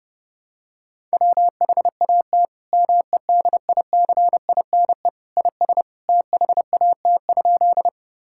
Morse code, heard as a keyed tone, about 32 wpm, 700 Hz. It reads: WHAT MEDICINE IS THAT?
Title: Sentences from Top 1000 Words - Review Part 6 - 30wpm